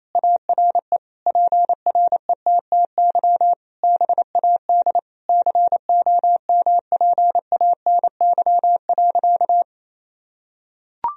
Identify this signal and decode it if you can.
Morse code, heard as a keyed tone, about 28 wpm, 700 Hz. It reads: ARE PRETTY BAD COMPANY.
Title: Sentences from Top 1000 Words - Review Part 3 - 28wpm